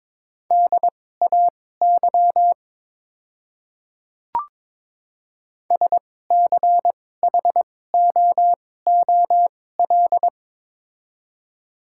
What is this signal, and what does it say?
Morse code, heard as a keyed tone, about 22 wpm, spaced as for 16 wpm, 700 Hz. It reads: DAY SCHOOL
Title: 2x Character Spacing - Top 300 Words - 22wpm